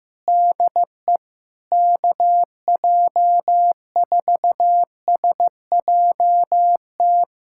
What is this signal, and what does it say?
Morse code, heard as a keyed tone, about 15 wpm, 700 Hz. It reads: DE KJ4SJT
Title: USA Callsign DE Callsign 15wpm